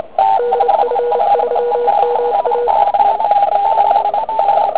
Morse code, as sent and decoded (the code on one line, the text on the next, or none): -.. ..-. ...-- -- .-
DF3MA